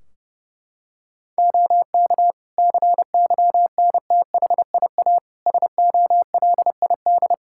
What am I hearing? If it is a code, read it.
Morse code, heard as a keyed tone, about 32 wpm, 700 Hz. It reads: OK CYNTHIA SOLID
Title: Practice QSO - Exchange 3 - More Info - 30wpm